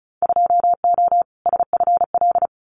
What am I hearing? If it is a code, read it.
Morse code, heard as a keyed tone, about 34 wpm, 700 Hz. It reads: 2O SFL
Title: Winter Field Day Exchanges 35wpm